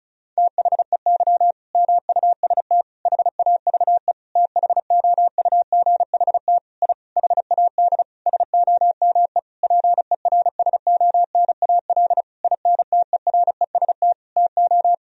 Morse code, heard as a keyed tone, about 36 wpm, 700 Hz. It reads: THEY MUST HAVE THOUGHT I HAD SOME PERSONAL INTEREST TO
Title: Sentences from Top 1000 Words - Review Part 4 - 35wpm